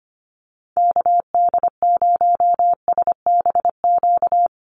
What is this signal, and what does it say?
Morse code, heard as a keyed tone, about 25 wpm, 700 Hz. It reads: KD0SBQ
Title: USA Call Signs Repeated 3x 25wpm